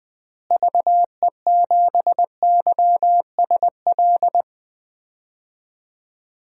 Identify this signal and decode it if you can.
Morse code, heard as a keyed tone, about 20 wpm, 700 Hz. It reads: VE7YSL
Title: DXCC-100 Call Signs - 6 Characters - Repeated 2x - 20wpm